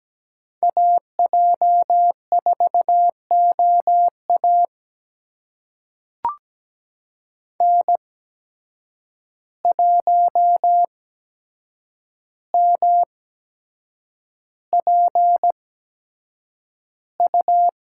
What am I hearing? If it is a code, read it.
Morse code, heard as a keyed tone, about 17 wpm, spaced as for 7 wpm, 700 Hz. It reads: AJ4OA N1MPU
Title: USA Call Signs - 5 Characters - 8x Character Spacing - 17wpm